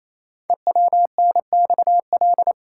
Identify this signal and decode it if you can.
Morse code, heard as a keyed tone, about 27 wpm, 700 Hz. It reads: EWNXL
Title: Code Groups - 5 Letters - 28wpm